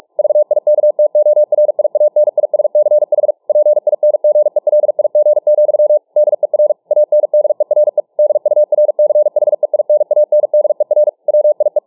Morse code, heard as a keyed tone, about 47 wpm, 600 Hz. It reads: VIKTORIANISCH PINGELIG, DER ANDERE DURCHEINANDER WIE
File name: dk5ke / Morsepsycho 45WpM